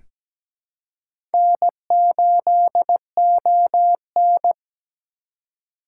Morse code, heard as a keyed tone, about 17 wpm, 700 Hz. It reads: N8ON
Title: USA Call Signs - 4 Characters - 2x Character Spacing - 17wpm